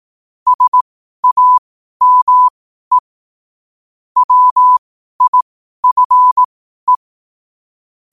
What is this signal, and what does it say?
Morse code, heard as a keyed tone, about 18 wpm, spaced as for 12 wpm, 1000 Hz. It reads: SAME WIFE
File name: qotdfiles / 12wpm